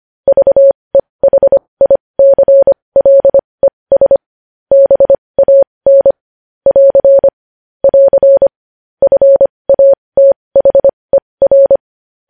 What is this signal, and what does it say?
Morse code, heard as a keyed tone, about 25 wpm, spaced as for 20 wpm, 550 Hz. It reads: VEHICLES BAN <AR> <AR> FATHER
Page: news headlines in morse code at 20 wpm